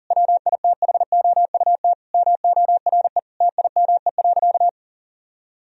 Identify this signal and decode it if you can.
Morse code, heard as a keyed tone, about 40 wpm, 700 Hz. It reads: WITHOUT MORE TIME.
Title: Sentences from Top 1000 Words - Review Part 3 - 40wpm